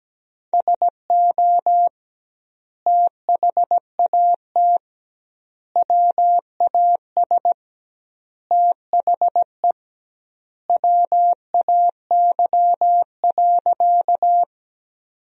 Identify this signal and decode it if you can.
Morse code, heard as a keyed tone, about 17 wpm, 700 Hz. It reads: SO THAT WAS THE WAY.
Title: 2x Word-Spacing - Sentences from Top 100 Words - 17wpm